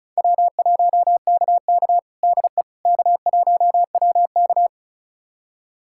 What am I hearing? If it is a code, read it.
Morse code, heard as a keyed tone, about 34 wpm, 700 Hz. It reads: W1KK DE K1WK
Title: USA Callsign DE Callsign - 4 Characters - 35wpm